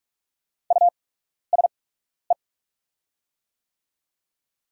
Morse code, heard as a keyed tone, about 44 wpm, spaced as for 12 wpm, 700 Hz. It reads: USE
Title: 8x Character Spacing - Top 100 Words - 45wpm